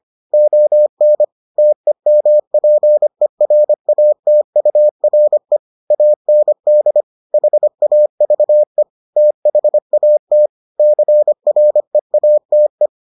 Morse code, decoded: ON TEMPERATURE AND HAVE THAT CREATE

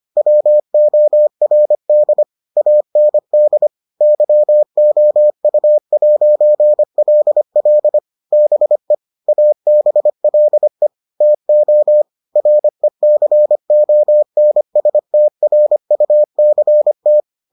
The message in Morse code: WORD AND YOU'LL BE ABLE TO RECONSTRUCT

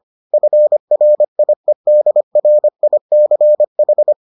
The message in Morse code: FRIEDRICH

